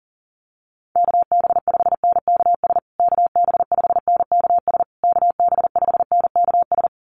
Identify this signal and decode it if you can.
Morse code, heard as a keyed tone, about 40 wpm, 700 Hz. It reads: KB5NKS KB5NKS KB5NKS
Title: USA Call Signs Repeated 3x 40wpm